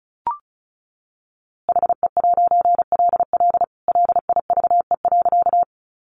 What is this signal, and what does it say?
Morse code, heard as a keyed tone, about 34 wpm, 700 Hz. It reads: HE'LL LIVE.
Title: A Sound Of Thunder 35wpm